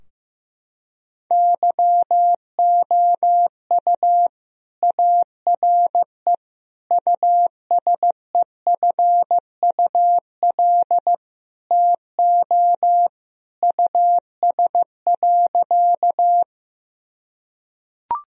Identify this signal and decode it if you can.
Morse code, heard as a keyed tone, about 16 wpm, 700 Hz. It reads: YOU ARE USEFUL TO US.
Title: Sentences from Top 1000 Words - Review Part 4 - 15wpm